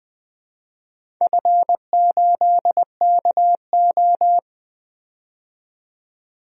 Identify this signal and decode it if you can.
Morse code, heard as a keyed tone, about 20 wpm, 700 Hz. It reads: F8KO
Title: DXCC-100 Call Signs - 4 Characters - 20wpm